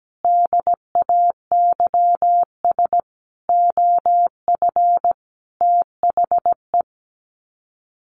Morse code, heard as a keyed tone, about 17 wpm, 700 Hz. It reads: DAYS OF THE
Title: Sets of 5 Words Encore 17wpm